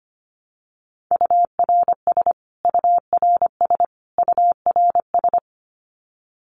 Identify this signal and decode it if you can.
Morse code, heard as a keyed tone, about 25 wpm, 700 Hz. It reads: URS URS URS
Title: Abbreviations Repeated 3x - 25wpm